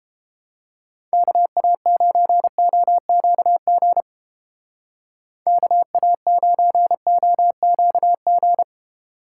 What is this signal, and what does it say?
Morse code, heard as a keyed tone, about 30 wpm, 700 Hz. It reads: KA9OQG KA9OQG